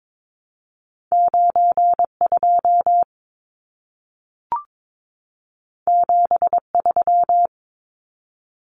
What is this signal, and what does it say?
Morse code, heard as a keyed tone, about 22 wpm, 700 Hz. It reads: 92 73
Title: Numbers - 2-Digit 22wpm